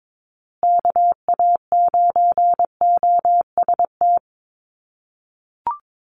Morse code, heard as a keyed tone, about 22 wpm, 700 Hz. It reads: KA9OST
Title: USA Call Signs Repeated 2x 22wpm